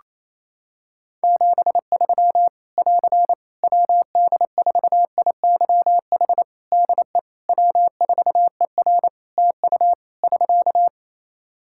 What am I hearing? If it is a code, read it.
Morse code, heard as a keyed tone, about 27 wpm, 700 Hz. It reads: 73 <AR> WD4IYH DE W4ER TU <SK>
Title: Practice QSO - Exchange 4 - Ending - 28wpm